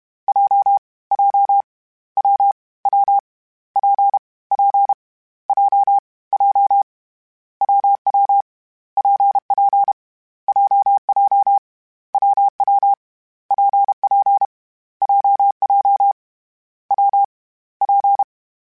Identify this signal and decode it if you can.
Morse code, heard as a keyed tone, about 32 wpm, 800 Hz. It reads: J J W W P P J J WW PP JJ WW PP JJ W P
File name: Variable Speed / Variable Speed-mp3s / WPJ WPJ1 sp @30